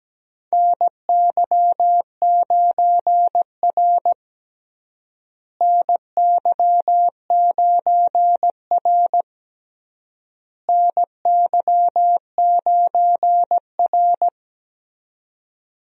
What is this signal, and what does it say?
Morse code, heard as a keyed tone, about 17 wpm, 700 Hz. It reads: NY9R NY9R NY9R
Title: USA Call Signs - 4 Characters - Repeated 3x - 17wpm